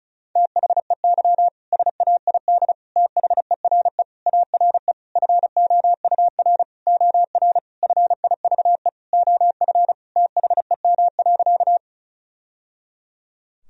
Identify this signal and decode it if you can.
Morse code, heard as a keyed tone, about 34 wpm, 700 Hz. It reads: THEY SAID THERE ARE FOUR OR FIVE OF THEM.
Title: Sentences from Top 1000 Words - Review Part 2 - 35wpm